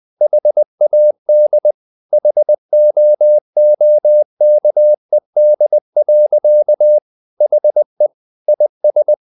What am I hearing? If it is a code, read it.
Morse code, heard as a keyed tone, about 20 wpm, 600 Hz. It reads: HAD HOOKED. HE IS
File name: the old man and the sea / the old man and the sea part 1 20wpm